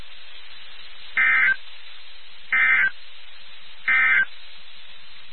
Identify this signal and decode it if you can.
Emergency Alert System (SAME) tones, heard as harsh buzzing bursts. It is heard from 1.0 to 4.3 s.
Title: NOAA Weather Radio WXK79